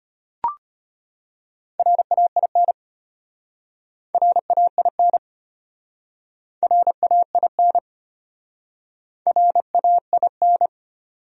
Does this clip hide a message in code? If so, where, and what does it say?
Yes: Morse code from 1.8 to 10.7 s, heard as a keyed tone, about 33 wpm, 700 Hz. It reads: RAIN RAIN RAIN RAIN